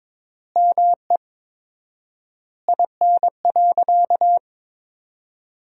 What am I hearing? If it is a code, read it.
Morse code, heard as a keyed tone, about 22 wpm, 700 Hz. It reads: ME IN.
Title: 4x Word-Spacing - Sentences from Top 300 Words - 22wpm